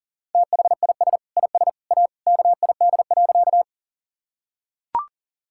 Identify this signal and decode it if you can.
Morse code, heard as a keyed tone, about 40 wpm, 700 Hz. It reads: THIS IS A KID.
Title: Sentences from Top 1600 Words 40wpm